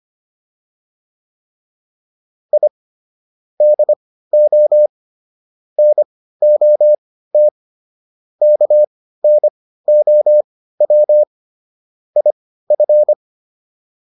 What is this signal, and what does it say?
Morse code, heard as a keyed tone, about 25 wpm, spaced as for 15 wpm, 600 Hz. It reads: I DO NOT KNOW IF